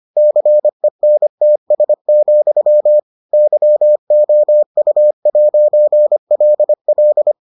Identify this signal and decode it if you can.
Morse code, heard as a keyed tone, about 25 wpm, 600 Hz. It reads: CENTS, YOU'LL